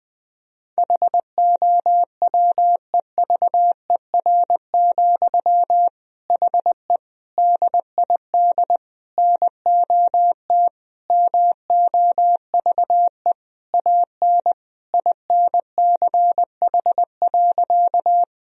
Morse code, heard as a keyed tone, about 21 wpm, 700 Hz. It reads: HOWEVER, HE DID NOT MOVE AN INCH.
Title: Sentences from Top 1000 Words - Review Part 3 - 20wpm